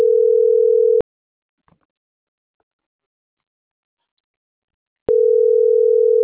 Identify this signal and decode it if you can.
Ringback tone (North America), 440 + 480 Hz.